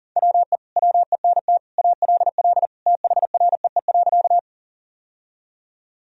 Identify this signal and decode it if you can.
Morse code, heard as a keyed tone, about 40 wpm, 700 Hz. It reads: WE WENT ALL THREE.
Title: Sentences from Top 1000 Words - Review Part 1 - 40wpm